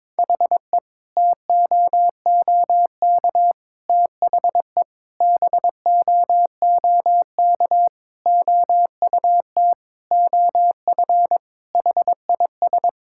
Morse code, heard as a keyed tone, about 22 wpm, 700 Hz. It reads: HE TOOK THE BOOK OUT OF HIS
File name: practice / 22wpm / Sentences with 3-Word Phrases - 22wpm